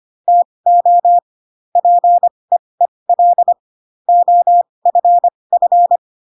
Morse code, heard as a keyed tone, about 25 wpm, spaced as for 20 wpm, 700 Hz. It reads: TO PEEL OFF
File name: Pinocchio / Pinocchio ch1 15 25 ik1hge